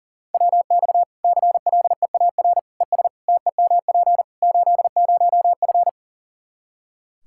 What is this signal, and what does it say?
Morse code, heard as a keyed tone, about 37 wpm, 700 Hz. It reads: WX CLEAR ES TEMP 80F
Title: K1USN SST QSO Practice 40wpm